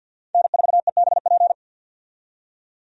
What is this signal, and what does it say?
Morse code, heard as a keyed tone, about 52 wpm, 700 Hz. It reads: N4EBP
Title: USA Call Signs - 5 Characters - 50wpm